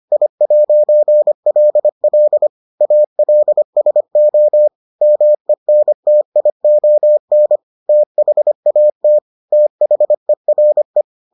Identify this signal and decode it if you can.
Morse code, heard as a keyed tone, about 25 wpm, 600 Hz. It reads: I'LL ALSO MENTION THAT THERE